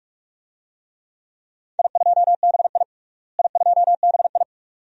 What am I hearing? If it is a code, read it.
Morse code, heard as a keyed tone, about 46 wpm, 700 Hz. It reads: I2BI I2BI
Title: DXCC-100 Call Signs - 4 Characters - Repeated 2x - 45wpm